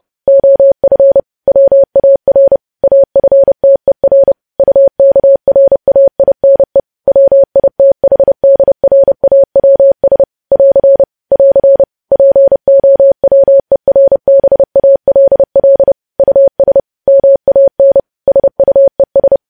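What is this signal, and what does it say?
Morse code, heard as a keyed tone, about 30 wpm, 550 Hz. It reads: OF WAR AFTER UKRAINE WITHDRAWS <AR> <AR> POWERBALL US MAN SUES
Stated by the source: News headlines sent as Morse practice at 30.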